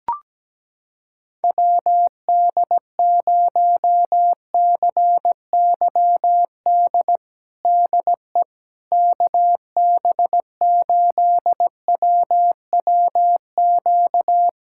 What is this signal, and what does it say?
Morse code, heard as a keyed tone, about 17 wpm, 700 Hz. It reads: WD0CYD DE KB8WWQ
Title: USA Callsign DE Callsign - 6 Characters - 17wpm